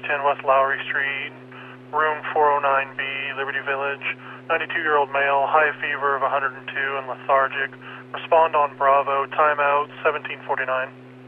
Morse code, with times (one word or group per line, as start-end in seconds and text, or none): none